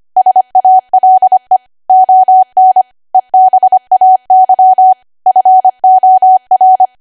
Morse code, heard as a keyed tone, about 25 wpm, 750 Hz. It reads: SALE ON EBAY FOR
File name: Archive / 25 WPM / 25WPM